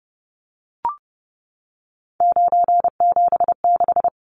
Morse code, heard as a keyed tone, about 30 wpm, 700 Hz. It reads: 976
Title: Numbers - 3-Digit 30wpm